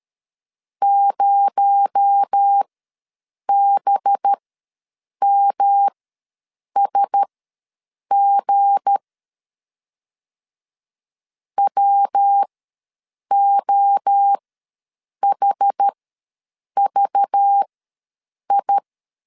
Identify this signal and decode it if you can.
Morse code, heard as a keyed tone, about 13 wpm, spaced as for 7 wpm, 800 Hz. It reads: 0BMSG WOHVI
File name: LEZIONI / 7 GF9Z / 7 GF9 ws 12wpm 3344